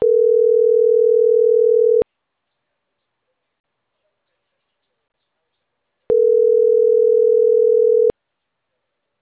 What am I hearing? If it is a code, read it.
Ringback tone (North America), 440 + 480 Hz.